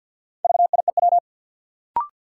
Morse code, heard as a keyed tone, about 50 wpm, 700 Hz. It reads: VIEW
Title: Sets of 5 Words - New Word Drill - 50wpm